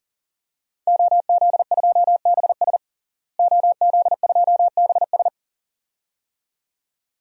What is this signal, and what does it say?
Morse code, heard as a keyed tone, about 40 wpm, 700 Hz. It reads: OZ2BS OZ2BS
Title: DXCC-100 Call Signs - 5 Characters - Repeated 2x - 40wpm